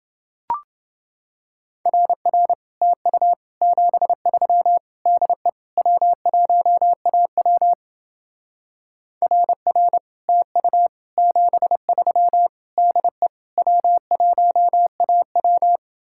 Morse code, heard as a keyed tone, about 28 wpm, 700 Hz. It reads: RR TU 73 DE W1AW RR TU 73 DE W1AW